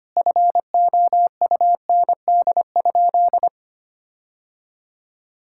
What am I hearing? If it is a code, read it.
Morse code, heard as a keyed tone, about 25 wpm, 700 Hz. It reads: FOUND?
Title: Sentences from Top 1600 Words 25wpm